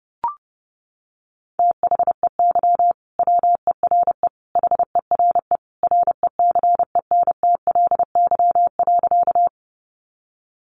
Morse code, heard as a keyed tone, about 30 wpm, 700 Hz. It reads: THEY WERE HERE RECENTLY.
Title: Sentences from Top 1200 Words 30wpm